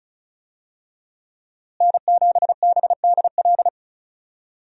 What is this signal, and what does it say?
Morse code, heard as a keyed tone, about 36 wpm, 700 Hz. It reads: N7BDL
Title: USA Call Signs - 5 Characters - Repeated 3x - 35wpm